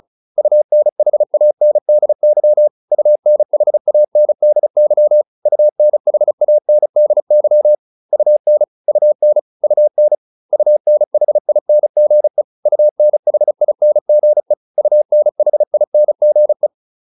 Morse code, heard as a keyed tone, about 34 wpm, 600 Hz. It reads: UNHANDY UNHANDY UNHANDY UN UN UN UNHINGE UNHINGE UNHINGE